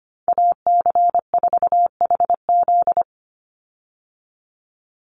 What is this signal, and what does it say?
Morse code, heard as a keyed tone, about 25 wpm, 700 Hz. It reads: AC4HZ
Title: USA Call Signs 25wpm